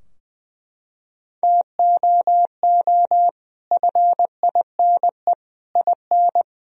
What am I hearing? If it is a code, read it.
Morse code, heard as a keyed tone, about 21 wpm, 700 Hz. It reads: TOO FINE IN
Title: Sentences from Top 1000 Words - Review Part 4 - 20wpm